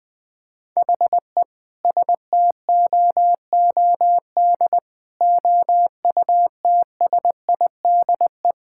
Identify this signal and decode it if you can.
Morse code, heard as a keyed tone, about 20 wpm, 700 Hz. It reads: HE STOOD OUTSIDE
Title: Sentences from Top 1600 Words 20wpm